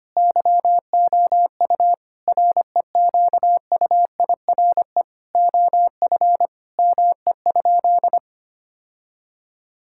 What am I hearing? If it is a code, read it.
Morse code, heard as a keyed tone, about 25 wpm, 700 Hz. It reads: YOU REQUIRE OF ME?
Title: Sentences from Top 1000 Words - Review Part 6 - 25wpm